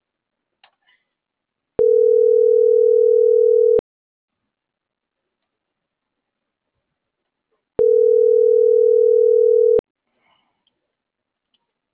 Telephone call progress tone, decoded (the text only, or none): ringback tone